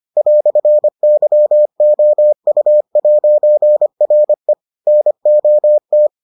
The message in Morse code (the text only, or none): "YOU'RE NOT